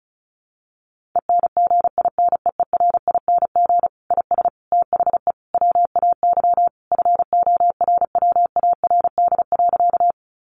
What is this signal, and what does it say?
Morse code, heard as a keyed tone, about 34 wpm, 700 Hz. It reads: ENGINEERING IS THE WAY FORWARD.
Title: Sentences from Top 1500 Words 35wpm